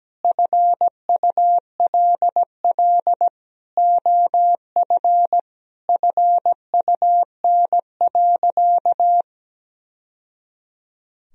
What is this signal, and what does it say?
Morse code, heard as a keyed tone, about 17 wpm, 700 Hz. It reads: FULL OF FUN.
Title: Sentences from Top 1000 Words - Review Part 4 - 17wpm